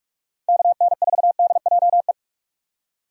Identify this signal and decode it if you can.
Morse code, heard as a keyed tone, about 46 wpm, 700 Hz. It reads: KN4DJE
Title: USA Call Signs - 6 Characters - 45wpm